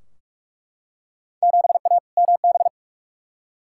Morse code, heard as a keyed tone, about 41 wpm, 700 Hz. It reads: 7A MB
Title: Field Day Exchanges Encore - 45wpm